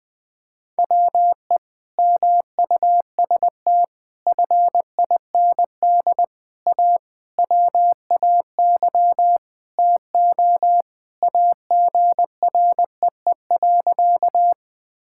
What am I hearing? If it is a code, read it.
Morse code, heard as a keyed tone, about 21 wpm, 700 Hz. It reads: WE MUST FIND A WAY TO AGREE.
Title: Sentences with 5-Word Phrases - Spoken Beginning - 20wpm